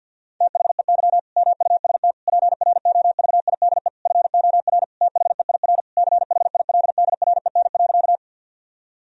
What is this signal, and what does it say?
Morse code, heard as a keyed tone, about 51 wpm, 700 Hz. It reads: THEY MUST PROVIDE FOR THEIR CHILDREN.
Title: Sentences from Top 1000 Words - Review Part 3 - 50wpm